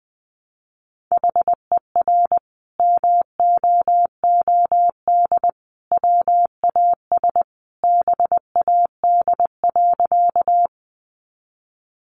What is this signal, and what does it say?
Morse code, heard as a keyed tone, about 20 wpm, 700 Hz. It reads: HER MOOD WAS BAD.
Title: Sentences from Top 1100 Words 20wpm